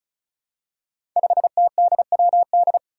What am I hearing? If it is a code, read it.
Morse code, heard as a keyed tone, about 36 wpm, 700 Hz. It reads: 5TDWD
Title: Five Challenging Letter-Number Combinations 35wpm